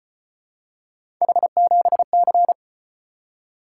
Morse code, heard as a keyed tone, about 35 wpm, 700 Hz. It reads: H7C